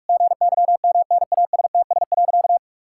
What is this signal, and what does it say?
Morse code, heard as a keyed tone, about 41 wpm, 700 Hz. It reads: GYMNASTS.
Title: Pangram Challenge v2 - 45wpm